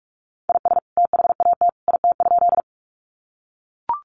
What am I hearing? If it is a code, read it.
Morse code, heard as a keyed tone, about 44 wpm, 700 Hz. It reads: IS THAT IT?
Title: Sentences from Top 100 Words 45wpm